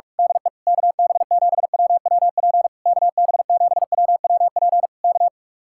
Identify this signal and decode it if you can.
Morse code, heard as a keyed tone, about 46 wpm, 700 Hz. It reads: DE KB7WWP KB7WWP K
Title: Speed-Racing Common QSO Phrases Encore - 45wpm